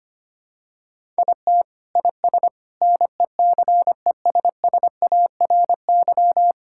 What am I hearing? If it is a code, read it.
Morse code, heard as a keyed tone, about 25 wpm, 700 Hz. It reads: IT IS NECESSARY
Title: Sentences from Top 1000 Words - Review Part 5 - 25wpm